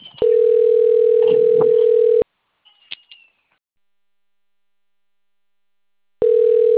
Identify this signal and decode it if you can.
Ringback tone (North America), 440 + 480 Hz.
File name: rg-1600-unknown- - - .54264